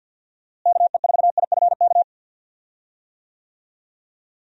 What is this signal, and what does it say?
Morse code, heard as a keyed tone, about 51 wpm, 700 Hz. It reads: KE4IFK
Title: USA Call Signs - 6 Characters - Repeated 3x - 50wpm